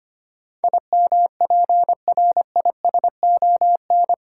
Morse code, heard as a keyed tone, about 25 wpm, 690 Hz. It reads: IMPRISON
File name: 8 Letter Words - First 3 Letters Given - 25wpm